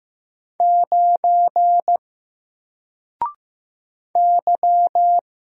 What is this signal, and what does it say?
Morse code, heard as a keyed tone, about 15 wpm, 700 Hz. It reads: 9 Y
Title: Single Letter-Number Alt 15wpm